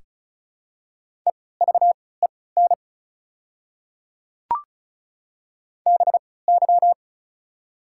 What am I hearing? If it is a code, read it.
Morse code, heard as a keyed tone, about 36 wpm, spaced as for 20 wpm, 700 Hz. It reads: EVEN BY
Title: 3x Character Spacing - Top 200 Words - 35wpm